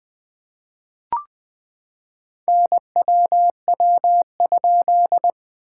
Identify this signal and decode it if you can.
Morse code, heard as a keyed tone, about 19 wpm, 700 Hz. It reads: NWW?
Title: SOTA QSO Practice 20wpm